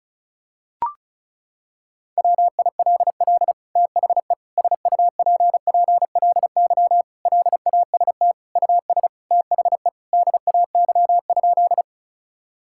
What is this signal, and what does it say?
Morse code, heard as a keyed tone, about 36 wpm, 700 Hz. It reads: WILL THE SUPPLY LAST US THE DAY?
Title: Sentences from Top 1000 Words - Review Part 5 - 35wpm